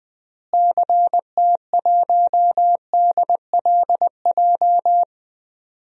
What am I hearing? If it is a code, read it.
Morse code, heard as a keyed tone, about 20 wpm, 700 Hz. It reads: CT1DLJ
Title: DXCC-100 Call Signs - 6 Characters - Repeated 3x - 20wpm